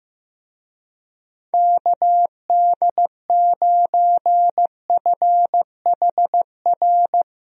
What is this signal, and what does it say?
Morse code, heard as a keyed tone, about 16 wpm, 700 Hz. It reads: KD9FHR
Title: USA Call Signs - 6 Characters - Repeated 2x - 15wpm